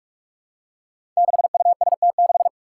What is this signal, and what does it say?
Morse code, heard as a keyed tone, about 46 wpm, 700 Hz. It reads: 6UST6
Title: Five Challenging Letter-Number Combinations 45wpm